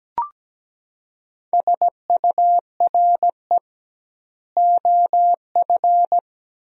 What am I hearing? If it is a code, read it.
Morse code, heard as a keyed tone, about 17 wpm, 700 Hz. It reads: SURE OF
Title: 2x Word-Spacing - Sentences from Top 300 Words - 17wpm